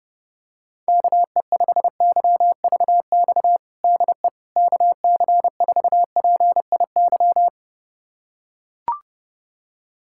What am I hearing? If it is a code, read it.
Morse code, heard as a keyed tone, about 30 wpm, 700 Hz. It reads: KE5YVX DE KC4PIY
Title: USA Callsign DE Callsign - 6 Characters - 30wpm